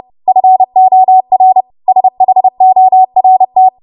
Morse code, heard as a keyed tone, about 30 wpm, 750 Hz. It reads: FOR SHORT
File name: Morse / 30WPM